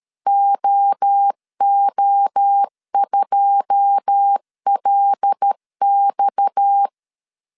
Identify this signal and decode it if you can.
Morse code, heard as a keyed tone, about 13 wpm, 800 Hz. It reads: OO2LX
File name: LEZIONI / 9 2 12wpm 2301